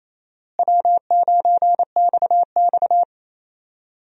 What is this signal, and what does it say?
Morse code, heard as a keyed tone, about 28 wpm, 700 Hz. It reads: W9XX
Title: USA Call Signs - 4 Characters - Repeated 3x - 28wpm